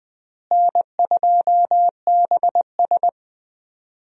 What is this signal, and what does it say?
Morse code, heard as a keyed tone, about 20 wpm, 700 Hz. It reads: N2BS
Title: USA Call Signs - 4 Characters - Repeated 3x - 20wpm